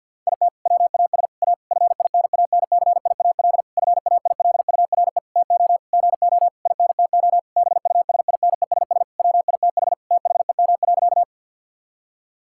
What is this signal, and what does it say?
Morse code, heard as a keyed tone, about 50 wpm, 700 Hz. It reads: IT WAS A FINANCIAL FAILURE TO GO INTO BUSINESS WITH THEM.
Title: Sentences from Top 1000 Words - Review Part 6 - 50wpm